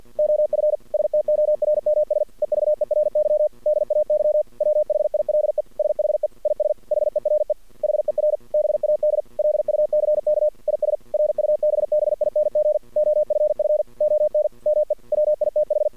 Morse code, heard as a keyed tone, about 51 wpm, 600 Hz. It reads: OF STORMS EVERY DAY WHILE SHE IS HERE HER BAD LUCK IS RUBBING OFF ON ME WITH
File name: archived / CRSnet